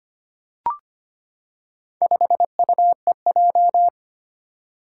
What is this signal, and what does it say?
Morse code, heard as a keyed tone, about 25 wpm, 700 Hz. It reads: 5UEJ
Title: Four Challenging Letter-Number Combinations 25wpm